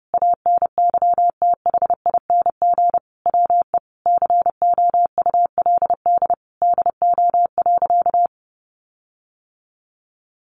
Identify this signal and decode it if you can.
Morse code, heard as a keyed tone, about 30 wpm, 700 Hz. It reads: ANYTHING WE COULD DO.
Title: Sentences from Top 1400 Words 30wpm